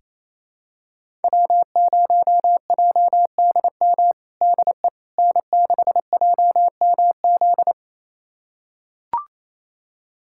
Morse code, heard as a keyed tone, about 28 wpm, 700 Hz. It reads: W0JDM DE N6JMZ
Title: USA Callsign DE Callsign 28wpm